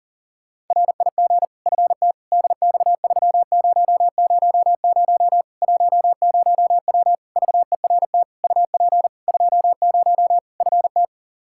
Morse code, heard as a keyed tone, about 40 wpm, 700 Hz. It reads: RIG FT DX3000 10W VERT UP 20 FT
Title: Practice QSO - Exchange 2 - Info - 40wpm